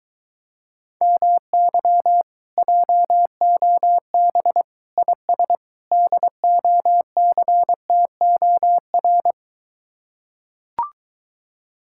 Morse code, heard as a keyed tone, about 23 wpm, 700 Hz. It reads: MY JOB IS DOCTOR